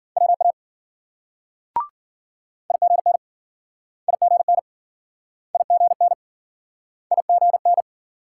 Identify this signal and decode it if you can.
Morse code, heard as a keyed tone, about 53 wpm, 700 Hz. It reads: WA IGN IGN IGN IGN